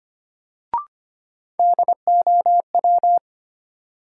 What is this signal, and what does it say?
Morse code, heard as a keyed tone, about 25 wpm, 700 Hz. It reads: DOW
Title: Sets of 5 Words - New Word Drill - 25wpm